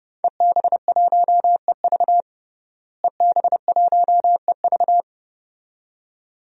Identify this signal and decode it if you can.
Morse code, heard as a keyed tone, about 30 wpm, 700 Hz. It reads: EB1EV EB1EV
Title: DXCC-100 Call Signs - 5 Characters - Repeated 2x - 30wpm